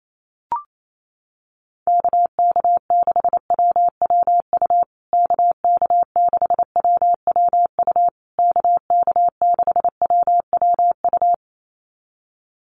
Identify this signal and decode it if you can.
Morse code, heard as a keyed tone, about 28 wpm, 700 Hz. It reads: KK6WWU KK6WWU KK6WWU
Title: USA Call Signs Repeated 3x 28wpm